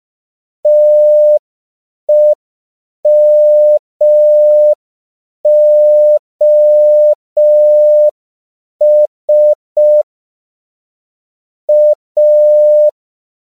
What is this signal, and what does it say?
Morse code, heard as a keyed tone, about 5 wpm, 600 Hz. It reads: TEMOS A
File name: CW / 5PPM